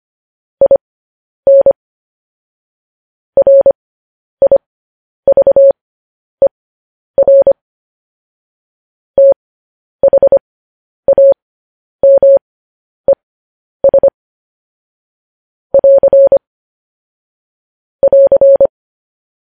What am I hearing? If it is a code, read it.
Morse code, heard as a keyed tone, about 25 wpm, spaced as for 10 wpm, 550 Hz. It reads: IN RIVER THAMES <AR> <AR>
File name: podcast / News headlines-1x-25x10